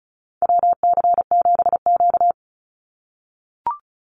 Morse code, heard as a keyed tone, about 36 wpm, 700 Hz. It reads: WC7Q
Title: Call Signs - 35wpm